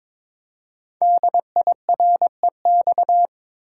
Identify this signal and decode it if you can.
Morse code, heard as a keyed tone, about 22 wpm, 700 Hz. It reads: DIREX
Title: Code Groups - 5 Letters - 22wpm